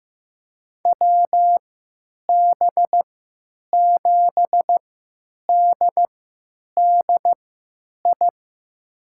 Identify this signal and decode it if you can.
Morse code, heard as a keyed tone, about 15 wpm, spaced as for 9 wpm, 700 Hz. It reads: WB7DDI
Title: USA Call Signs - 6 Characters - 3x Character Spacing - 15wpm